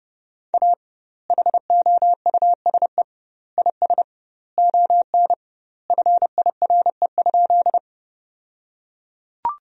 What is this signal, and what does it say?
Morse code, heard as a keyed tone, about 32 wpm, 700 Hz. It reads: A HOUSE IS ON FIRE?
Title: 2x Word-Spacing - Sentences from Top 400 Words - 30wpm